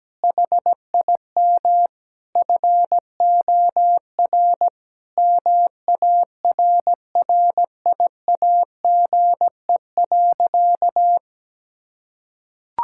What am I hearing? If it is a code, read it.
Morse code, heard as a keyed tone, about 17 wpm, 700 Hz. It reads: HIM FOR MARRIAGE.
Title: Sentences from Top 1000 Words - Review Part 6 - 17wpm